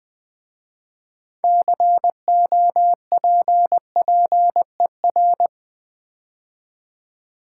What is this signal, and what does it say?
Morse code, heard as a keyed tone, about 20 wpm, 700 Hz. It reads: COPPER
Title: 6 Letter Words - First 3 Letters Given - 20wpm